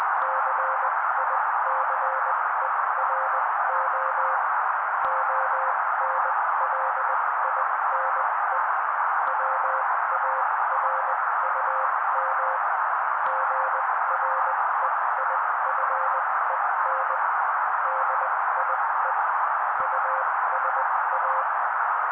Morse code, heard as a keyed tone, about 20 wpm, spaced as for 16 wpm, 530 Hz. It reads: CICERO ONLINE WARUM GREIFEN DIE USA
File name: daily / 16wpm noise